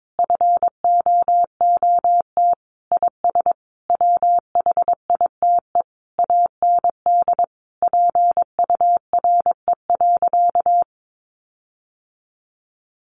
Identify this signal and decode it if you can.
Morse code, heard as a keyed tone, about 22 wpm, 700 Hz. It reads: FOOT IS WHITE AND PURE.
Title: Sentences from Top 1500 Words 22wpm